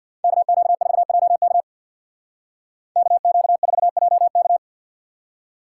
Morse code, heard as a keyed tone, about 59 wpm, 700 Hz. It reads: KQ4JK KQ4JK
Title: Speed Racing - USA 2x2 Call Signs 50wpm